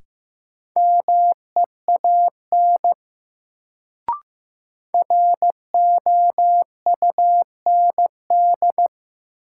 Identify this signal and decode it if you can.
Morse code, heard as a keyed tone, about 15 wpm, 700 Hz. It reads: MEAN ROUND
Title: Sentences from Top 200 Words - New Word Drill - 15wpm